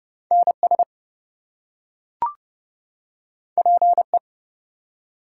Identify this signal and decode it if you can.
Morse code, heard as a keyed tone, about 28 wpm, 700 Hz. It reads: NS PE
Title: US States and Canadian Provinces 30wpm